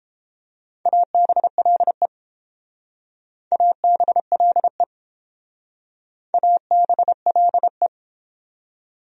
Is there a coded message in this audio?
Yes: Morse code from 0.9 to 7.9 s, heard as a keyed tone, about 29 wpm, 700 Hz. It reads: ABLE ABLE ABLE